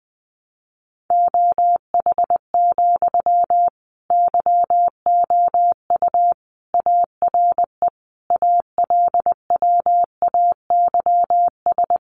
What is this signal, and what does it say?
Morse code, heard as a keyed tone, about 20 wpm, 700 Hz. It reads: OH, YOU ARE ALWAYS
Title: Sentences from Top 900 Words 20wpm